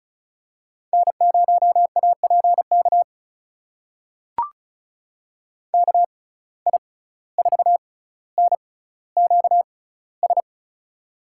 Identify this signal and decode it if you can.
Morse code, heard as a keyed tone, about 36 wpm, spaced as for 17 wpm, 700 Hz. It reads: N0APK KI4NQS
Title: USA Call Signs - 6x Character Spacing - 35wpm